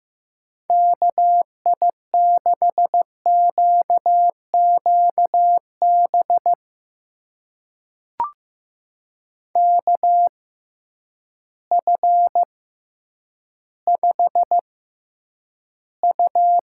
Morse code, heard as a keyed tone, about 16 wpm, spaced as for 9 wpm, 700 Hz. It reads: KI6QQB KF5U
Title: USA Call Signs - 6 Characters - 6x Character Spacing - 15wpm